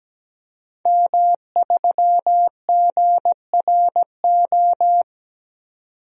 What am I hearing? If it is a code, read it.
Morse code, heard as a keyed tone, about 17 wpm, 700 Hz. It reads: M3GRO
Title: DXCC-100 Call Signs - 5 Characters - 17wpm